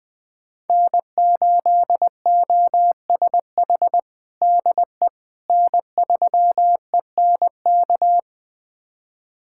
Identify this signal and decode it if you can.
Morse code, heard as a keyed tone, about 20 wpm, 700 Hz. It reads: N8OSH DE N3ENK
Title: USA Callsign DE Callsign - 5 Characters - 20wpm